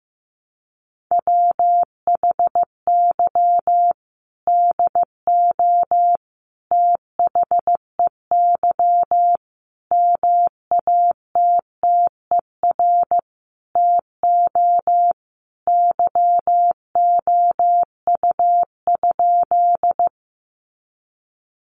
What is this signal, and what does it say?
Morse code, heard as a keyed tone, about 15 wpm, 700 Hz. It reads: WHY DO THEY MATTER TO YOU?
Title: Sentences from Top 700 Words 15wpm